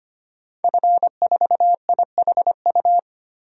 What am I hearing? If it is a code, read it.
Morse code, heard as a keyed tone, about 25 wpm, 700 Hz. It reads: F4IHU
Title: DXCC-100 Call Signs - 5 Characters - 25wpm